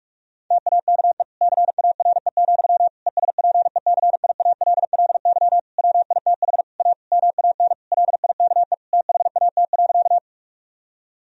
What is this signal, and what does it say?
Morse code, heard as a keyed tone, about 46 wpm, 700 Hz. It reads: TAKE CARE, ESPECIALLY WITH A MAN LIKE THAT.
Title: Sentences from Top 1000 Words - Review Part 4 - 45wpm